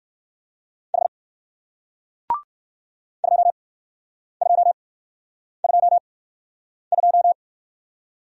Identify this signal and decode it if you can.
Morse code, heard as a keyed tone, about 59 wpm, spaced as for 9 wpm, 700 Hz. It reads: H 2222